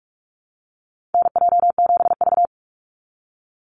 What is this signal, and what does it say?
Morse code, heard as a keyed tone, about 46 wpm, 700 Hz. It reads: NJ7V
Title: NA SOTA Chaser Callsigns 45wpm